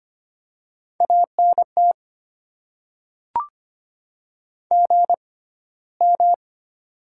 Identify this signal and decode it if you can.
Morse code, heard as a keyed tone, about 25 wpm, spaced as for 15 wpm, 700 Hz. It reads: ANT GM
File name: Top 20 QSO Elements - 6x Character Spacing - 25wpm